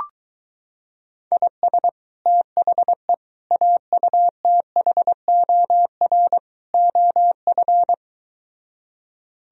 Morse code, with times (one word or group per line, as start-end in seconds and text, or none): 1.3-1.9 IS
2.3-3.1 THE
3.5-6.4 AUTHOR
6.7-7.9 OF